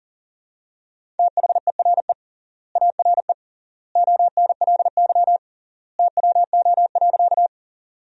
Morse code, heard as a keyed tone, about 40 wpm, 700 Hz. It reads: THERE ARE ONLY TWO.
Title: 3x Word-Spacing - Sentences from Top 200 Words - 40wpm